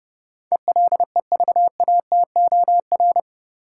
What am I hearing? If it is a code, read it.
Morse code, heard as a keyed tone, about 30 wpm, 700 Hz. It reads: ELEVATOR
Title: Sentences from Top 1600 Words - New Word Drill Alt - 30wpm